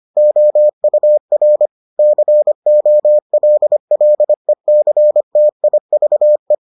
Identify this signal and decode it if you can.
Morse code, heard as a keyed tone, about 25 wpm, 600 Hz. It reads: OUR COLLECTIVE